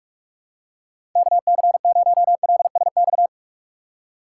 Morse code, heard as a keyed tone, about 42 wpm, 700 Hz. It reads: KC0LSX
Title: USA Call Signs - 6 Characters - 3x Character Spacing - 45wpm